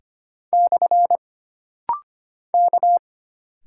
Morse code, heard as a keyed tone, about 25 wpm, 700 Hz. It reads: / K
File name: Single Character (Letter:Number:Punctuation:Prosign) - Rapid-Fire - 25wpm